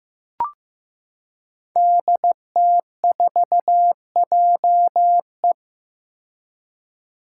Morse code, heard as a keyed tone, about 15 wpm, 700 Hz. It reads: DT4JE
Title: Five Challenging Letter-Number Combinations 15wpm